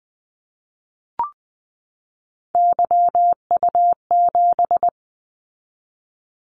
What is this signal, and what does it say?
Morse code, heard as a keyed tone, about 20 wpm, 700 Hz. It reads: YU7
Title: Callsign Prefixes for Top 25 DXCC - 20wpm